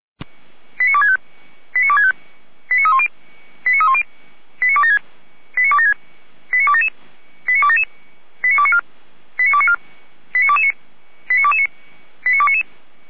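Five-tone selective call (ZVEI), heard as quick runs of five tones, tones of about 70 ms. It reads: 89265 89265 89310 89310 89276 89276 89277 89277 89294 89294 89229 89229 89299
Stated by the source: ZVEI five-tone selective call received by SDR, Feuerwehr Orlamünde (f85360).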